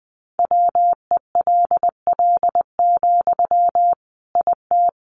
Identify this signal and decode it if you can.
Morse code, heard as a keyed tone, about 19 wpm, 700 Hz. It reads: WELL, IT
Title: Sentences from Top 1500 Words 20wpm